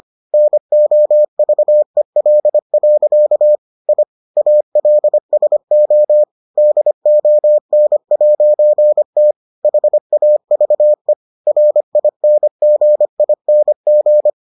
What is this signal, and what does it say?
Morse code, heard as a keyed tone, about 25 wpm, 600 Hz. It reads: NOVEL. I ALSO DON'T HAVE RINGING